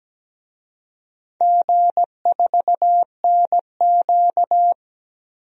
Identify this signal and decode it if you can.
Morse code, heard as a keyed tone, about 17 wpm, 700 Hz. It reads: G4NQ
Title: DXCC-100 Call Signs - 4 Characters - 17wpm